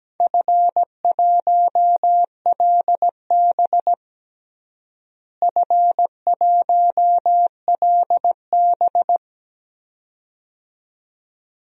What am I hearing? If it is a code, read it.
Morse code, heard as a keyed tone, about 17 wpm, 700 Hz. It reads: F1LB F1LB
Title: DXCC-100 Call Signs - 4 Characters - Repeated 2x - 17wpm